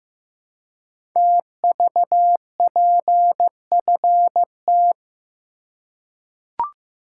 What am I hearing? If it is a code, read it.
Morse code, heard as a keyed tone, about 15 wpm, 700 Hz. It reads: TVPFT
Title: Code Groups - 5 Letters - 15wpm